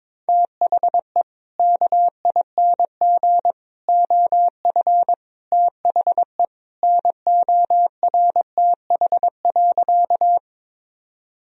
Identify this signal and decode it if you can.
Morse code, heard as a keyed tone, about 22 wpm, 700 Hz. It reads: THE KING OF THE NORTH.
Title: Sentences from Top 1000 Words - Review Part 1 - 22wpm